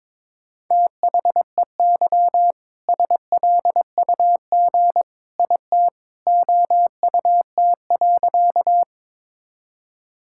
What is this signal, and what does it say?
Morse code, heard as a keyed tone, about 22 wpm, 700 Hz. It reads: THEY SLUG IT OUT.
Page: morse-code-ninja-practice-22wpm